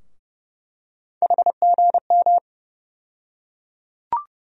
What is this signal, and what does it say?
Morse code, heard as a keyed tone, about 30 wpm, 700 Hz. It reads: HGM
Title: USA Call Sign Postfixes - 3 Letters 30wpm